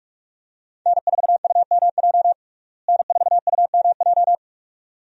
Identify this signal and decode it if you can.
Morse code, heard as a keyed tone, about 45 wpm, 700 Hz. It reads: N4UMJ N4UMJ
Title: USA Call Signs - 5 Characters - Repeated 2x - 45wpm